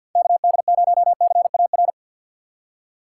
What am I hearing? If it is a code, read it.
Morse code, heard as a keyed tone, about 48 wpm, 700 Hz. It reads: KD0CAR
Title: USA Call Signs - 6x Character Spacing - 50wpm